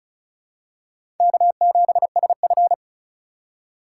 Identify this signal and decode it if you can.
Morse code, heard as a keyed tone, about 36 wpm, 700 Hz. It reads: K7SF
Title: K1USN SST QSO Practice - 35wpm